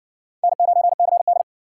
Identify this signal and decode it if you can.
Morse code, heard as a keyed tone, about 60 wpm, 700 Hz. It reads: N9CD